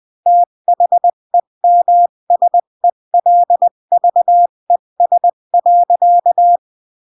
Morse code, heard as a keyed tone, about 20 wpm, 700 Hz. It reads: THEMSELVES.
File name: CW / jodorowsky / 82 commandments 20-15